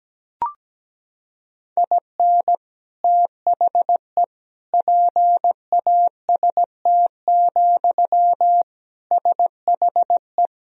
Morse code, heard as a keyed tone, about 17 wpm, 700 Hz. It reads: IN THE PAST, SHE
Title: Sentences from Top 1000 Words - Review Part 3 - 17wpm